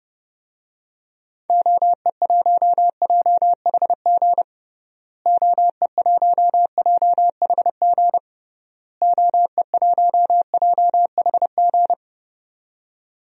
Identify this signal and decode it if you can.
Morse code, heard as a keyed tone, about 29 wpm, 700 Hz. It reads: OE1JHG OE1JHG OE1JHG
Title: DXCC-100 Call Signs - 6 Characters - Repeated 3x - 30wpm